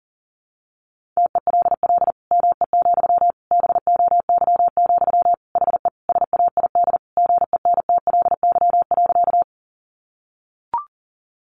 Morse code, heard as a keyed tone, about 40 wpm, 700 Hz. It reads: TELL ME, BOY, HE SAID GENTLY.
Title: Sentences from Top 1500 Words 40wpm